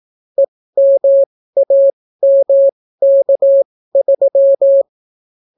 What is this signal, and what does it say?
Morse code, heard as a keyed tone, about 18 wpm, spaced as for 14 wpm, 550 Hz. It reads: EMAMK3